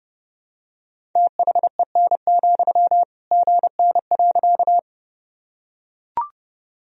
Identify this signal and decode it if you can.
Morse code, heard as a keyed tone, about 30 wpm, 700 Hz. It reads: THEN, GN.
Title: Sentences from Top 300 Words and Abbreviations 30wpm